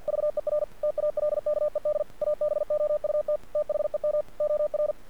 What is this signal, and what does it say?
Morse code, heard as a keyed tone, about 48 wpm, 600 Hz. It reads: VER TALKED ABOUT THEM OR
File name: archived / CRSnet